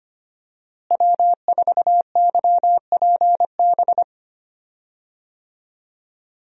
Morse code, heard as a keyed tone, about 25 wpm, 700 Hz. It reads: W4YPB
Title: USA Call Signs - 5 Characters - Repeated 2x - 25wpm